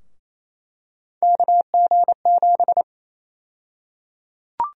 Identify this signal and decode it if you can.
Morse code, heard as a keyed tone, about 28 wpm, 700 Hz. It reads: KG7
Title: USA Call Sign Prefixes - 28wpm